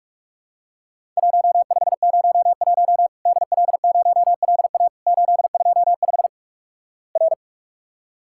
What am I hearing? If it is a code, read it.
Morse code, heard as a keyed tone, about 48 wpm, 700 Hz. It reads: 1501 DL0LA 825 R
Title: Worked All Europe DX Contest - QTC Exchange - 45wpm